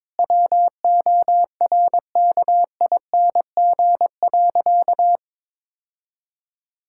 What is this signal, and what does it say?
Morse code, heard as a keyed tone, about 22 wpm, 700 Hz. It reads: WORKING.
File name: practice / 22wpm / Sentences with 3-Word Phrases - 22wpm